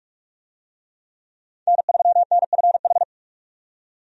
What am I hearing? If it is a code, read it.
Morse code, heard as a keyed tone, about 41 wpm, 700 Hz. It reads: N3NFH
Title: USA Call Signs - 5 Characters - 45wpm